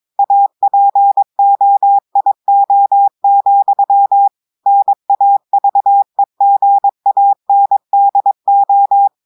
Morse code, heard as a keyed tone, about 22 wpm, 800 Hz. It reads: APOIO, NAVEGANDO
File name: audios / portugais / portugais texte en clair 1 vitesse 22 mots minutes